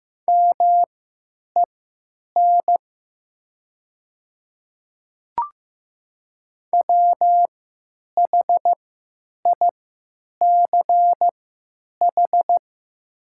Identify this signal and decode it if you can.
Morse code, heard as a keyed tone, about 15 wpm, spaced as for 9 wpm, 700 Hz. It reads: MEN WHICH
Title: 3x Character Spacing - Top 200 Words - 15wpm